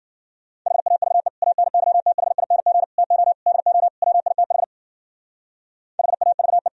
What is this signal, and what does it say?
Morse code, heard as a keyed tone, about 48 wpm, 700 Hz. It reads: HAVE ANYTHING TO DO WITH HAVE
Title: Speed-Racing - Sets of 5 Words 40wpm